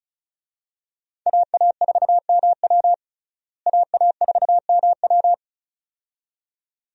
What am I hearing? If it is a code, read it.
Morse code, heard as a keyed tone, about 36 wpm, 700 Hz. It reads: AA4MW AA4MW
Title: USA Call Signs - 5 Characters - Repeated 2x - 35wpm